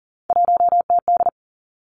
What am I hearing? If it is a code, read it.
Morse code, heard as a keyed tone, about 41 wpm, 700 Hz. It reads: 1TD
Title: Three Challenging Letter-Number Combinations 40wpm